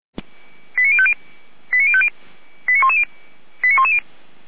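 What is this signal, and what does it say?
Five-tone selective call (ZVEI), heard as quick runs of five tones, tones of about 70 ms. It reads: 89050 89050 89110 89110
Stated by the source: Five-tone ZVEI selective call received by SDR, from the feuerwehr-orlamuende.org fire brigade audio server.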